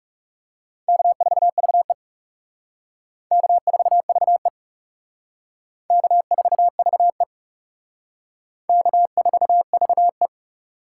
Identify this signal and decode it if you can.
Morse code, heard as a keyed tone, about 35 wpm, 700 Hz. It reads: K4VE K4VE K4VE K4VE
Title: Speed Racing - USA 1x2 Call Signs 30wpm